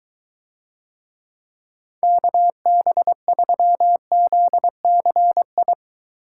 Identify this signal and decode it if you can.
Morse code, heard as a keyed tone, about 23 wpm, 700 Hz. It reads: KB3ZCI